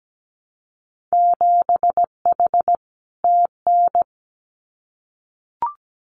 Morse code, heard as a keyed tone, about 17 wpm, 700 Hz. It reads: 7H TN
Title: Winter Field Day Exchanges Encore 17wpm